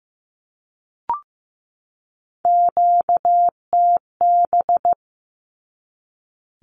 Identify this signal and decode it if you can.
Morse code, heard as a keyed tone, about 15 wpm, 700 Hz. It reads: QTB
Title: Q-Codes 15wpm